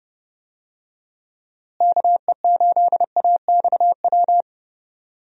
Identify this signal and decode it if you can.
Morse code, heard as a keyed tone, about 30 wpm, 700 Hz. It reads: KE8AXW
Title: USA Call Signs - 6 Characters - Repeated 3x - 30wpm